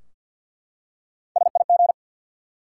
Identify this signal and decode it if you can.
Morse code, heard as a keyed tone, about 52 wpm, 700 Hz. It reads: SIG
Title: Top 90 QSO Elements 50wpm